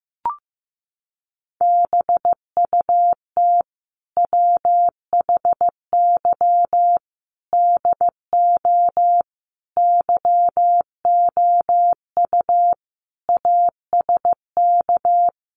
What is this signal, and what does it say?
Morse code, heard as a keyed tone, about 15 wpm, 700 Hz. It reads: BUT WHY DO YOU ASK
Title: Sentences from Top 200 Words 15wpm